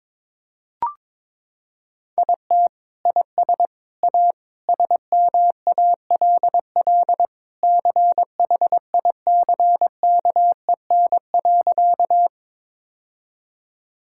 Morse code, heard as a keyed tone, about 22 wpm, 700 Hz. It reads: IT IS A SMALL CHICKEN.
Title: Sentences from Top 1000 Words - Review Part 5 - 22wpm